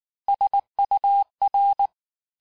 Morse code, heard as a keyed tone, about 19 wpm, 800 Hz. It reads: SUR
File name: cours-du-lundi files / cours-1595-teste-CHIFFRE-vitesse18-MPM